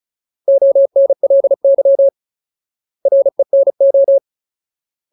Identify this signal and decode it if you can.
Morse code, heard as a keyed tone, about 34 wpm, 550 Hz. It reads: ONLY RENO
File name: Ultimate Word List / 35WPM Ultimate